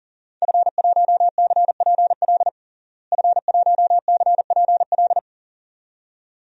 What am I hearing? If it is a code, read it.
Morse code, heard as a keyed tone, about 40 wpm, 700 Hz. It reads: F1CPL F1CPL
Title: DXCC-100 Call Signs - 5 Characters - Repeated 2x - 40wpm